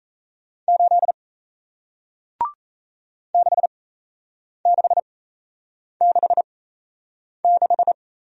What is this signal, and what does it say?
Morse code, heard as a keyed tone, about 38 wpm, spaced as for 8 wpm, 700 Hz. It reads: 8 6666